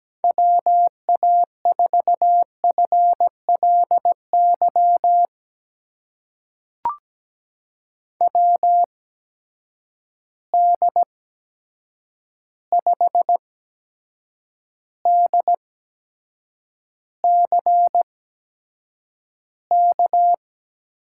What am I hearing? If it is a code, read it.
Morse code, heard as a keyed tone, about 17 wpm, spaced as for 7 wpm, 700 Hz. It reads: WA4FLY WD5DCK
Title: USA Call Signs - 6 Characters - 8x Character Spacing - 17wpm